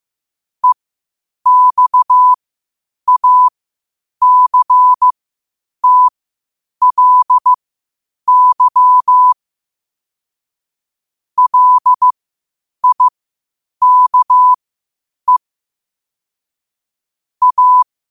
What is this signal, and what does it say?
Morse code, heard as a keyed tone, about 15 wpm, spaced as for 8 wpm, 1000 Hz. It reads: EXACTLY LIKE A
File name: qotdfiles / 8wpm